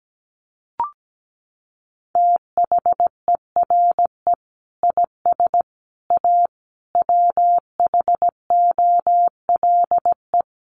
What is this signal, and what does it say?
Morse code, heard as a keyed tone, about 17 wpm, 700 Hz. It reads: THERE IS A WHOLE
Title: Sentences from Top 1400 Words 17wpm